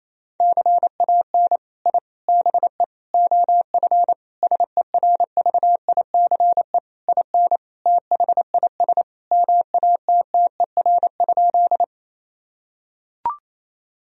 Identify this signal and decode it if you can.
Morse code, heard as a keyed tone, about 28 wpm, 700 Hz. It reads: CAN I BE OF SERVICE IN THIS MATTER?
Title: Sentences from Top 1000 Words - Review Part 4 - 28wpm